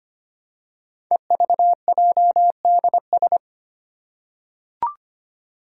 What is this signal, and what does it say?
Morse code, heard as a keyed tone, about 25 wpm, 700 Hz. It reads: EVJDS
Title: Code Groups - 5 Letters - 25wpm